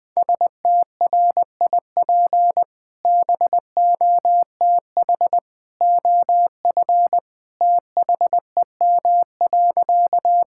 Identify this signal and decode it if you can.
Morse code, heard as a keyed tone, about 21 wpm, 700 Hz. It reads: STRIP BOTH OF THEM.
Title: Sentences from Top 1600 Words 20wpm